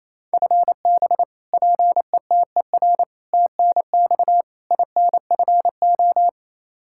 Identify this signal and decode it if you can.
Morse code, heard as a keyed tone, about 28 wpm, 700 Hz. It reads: FB PETER TNX INFO
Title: Practice QSO - Exchange 3 - More Info - 28wpm